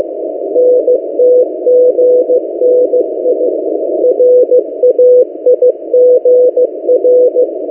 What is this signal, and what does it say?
Morse code, heard as a keyed tone, about 15 wpm, 500 Hz. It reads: NTGNI RAIGR
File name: M08a- -1355utc-8096khz